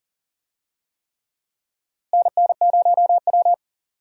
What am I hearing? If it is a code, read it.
Morse code, heard as a keyed tone, about 40 wpm, 700 Hz. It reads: NN0W
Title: USA Call Signs - 4 Characters - Repeated 3x - 40wpm